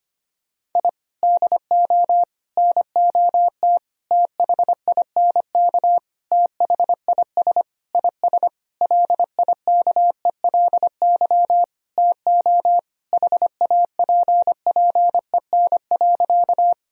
Morse code, heard as a keyed tone, about 25 wpm, 700 Hz. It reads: I DO NOT THINK THIS IS LIKELY TO HAPPEN.
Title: Sentences from Top 1000 Words - Review Part 4 - 25wpm